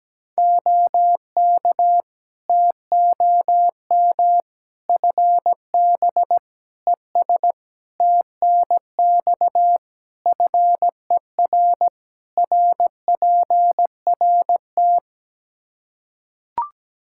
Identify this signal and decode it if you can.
Morse code, heard as a keyed tone, about 17 wpm, 700 Hz. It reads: OK TOM FB ES TNX FER RPRT
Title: Practice QSO - Exchange 2 - Info - 17wpm